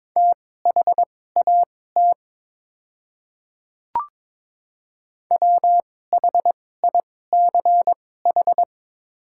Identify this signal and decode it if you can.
Morse code, heard as a keyed tone, about 22 wpm, spaced as for 16 wpm, 700 Hz. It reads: THAT WHICH
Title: 2x Character Spacing - Top 100 Words - 22wpm